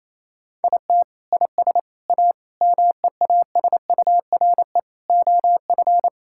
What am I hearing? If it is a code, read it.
Morse code, heard as a keyed tone, about 28 wpm, 700 Hz. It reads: IT IS A MEASURE OF
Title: Sentences from Top 1600 Words 28wpm